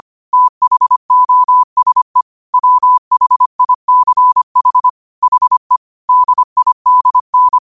Morse code, heard as a keyed tone, about 25 wpm, 1000 Hz. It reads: THOSE WHICH HE DIDN
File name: qotdfiles / 25wpm